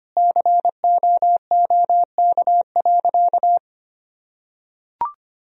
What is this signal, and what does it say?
Morse code, heard as a keyed tone, about 25 wpm, 700 Hz. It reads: COOK.
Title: Sentences from Top 1000 Words - Review Part 4 - 25wpm